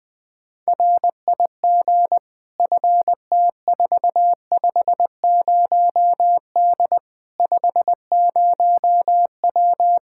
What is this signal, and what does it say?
Morse code, heard as a keyed tone, about 20 wpm, 700 Hz. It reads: RIG FT450D 50W
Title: Practice QSO - Exchange 2 - Info - 20wpm